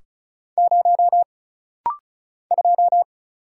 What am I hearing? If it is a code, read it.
Morse code, heard as a keyed tone, about 34 wpm, 700 Hz. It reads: 0 2
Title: Single Letter-Number - Rapid-Fire 35wpm